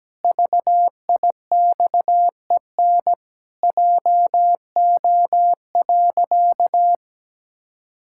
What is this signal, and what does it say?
Morse code, heard as a keyed tone, about 17 wpm, 700 Hz. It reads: VIXEN JO.
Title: Pangram Challenge v2 - 17wpm